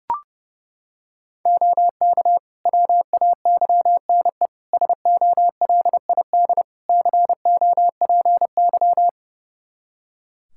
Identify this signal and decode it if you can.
Morse code, heard as a keyed tone, about 30 wpm, 700 Hz. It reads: OK WAYNE SOLID COPY
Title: Practice QSO - Exchange 3 - More Info - 30wpm